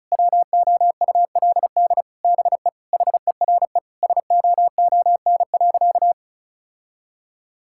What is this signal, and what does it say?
Morse code, heard as a keyed tone, about 34 wpm, 700 Hz. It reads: WOULD BE HERE SOON.
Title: Sentences from Top 1600 Words 35wpm